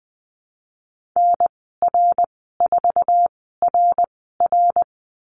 Morse code, heard as a keyed tone, about 20 wpm, spaced as for 15 wpm, 700 Hz. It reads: NR4RR
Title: 062 Lesson 11g - Letter h - Call Signs 20wpm